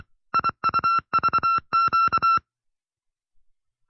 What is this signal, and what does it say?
Morse code, heard as a keyed tone, about 26 wpm, 1350 Hz. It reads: IUVQ